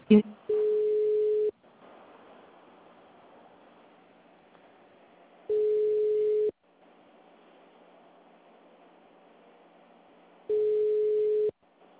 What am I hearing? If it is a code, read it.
Ringback tone (Europe, 425 Hz), with speech.